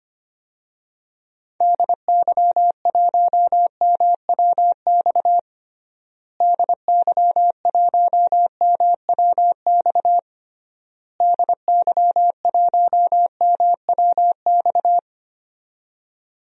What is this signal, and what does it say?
Morse code, heard as a keyed tone, about 25 wpm, 700 Hz. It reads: DY1MWX DY1MWX DY1MWX
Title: DXCC-100 Call Signs - 6 Characters - Repeated 3x - 25wpm